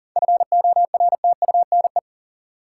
Morse code, heard as a keyed tone, about 40 wpm, 700 Hz. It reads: FORTUNE
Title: Seven Letter Words - 40wpm